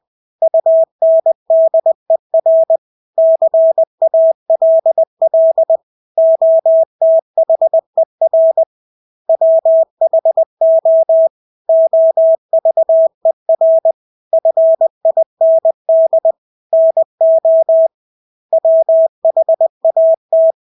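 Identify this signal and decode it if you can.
Morse code, heard as a keyed tone, about 20 wpm, 650 Hz. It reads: UNDER CALL OTHER WHO OVER FIND NO WHAT
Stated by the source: Morse practice of English words at 20 wpm.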